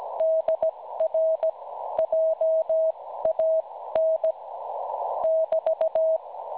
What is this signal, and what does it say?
Morse code, heard as a keyed tone, about 17 wpm, spaced as for 14 wpm, 640 Hz. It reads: DR JAN <BT>
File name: cwronde / websdr recording .6kHz